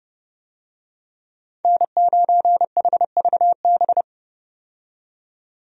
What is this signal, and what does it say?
Morse code, heard as a keyed tone, about 31 wpm, 700 Hz. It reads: N9HVB
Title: USA Call Signs - 5 Characters - 30wpm